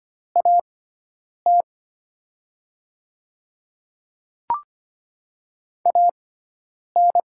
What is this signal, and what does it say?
Morse code, heard as a keyed tone, about 25 wpm, spaced as for 9 wpm, 700 Hz. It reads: AT AN
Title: 6x Character Spacing - Top 300 Words - 25wpm